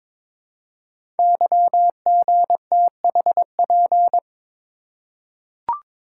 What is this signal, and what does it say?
Morse code, heard as a keyed tone, about 22 wpm, 700 Hz. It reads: YGTHP
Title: Code Groups - 5 Letters - 22wpm